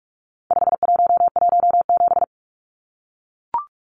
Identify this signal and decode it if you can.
Morse code, heard as a keyed tone, about 44 wpm, 700 Hz. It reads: 5117
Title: Numbers - 4-Digit 45wpm